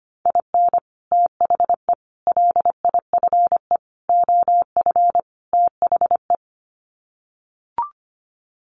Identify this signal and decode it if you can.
Morse code, heard as a keyed tone, about 25 wpm, 700 Hz. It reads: IN THE LIFE OF THE
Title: Sets of 5 Words Encore2 25wpm